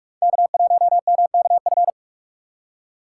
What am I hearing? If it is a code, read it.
Morse code, heard as a keyed tone, about 43 wpm, 700 Hz. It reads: K1MKF
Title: USA Call Signs - 5 Characters - 2x Character Spacing - 45wpm